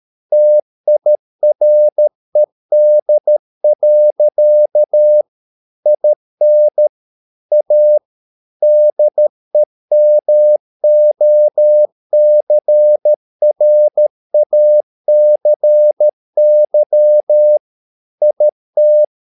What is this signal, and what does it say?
Morse code, heard as a keyed tone, about 13 wpm, 600 Hz. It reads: TIRED. IN A DEMOCRACY IT
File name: mp3 files / Lexiphiles - 13wpm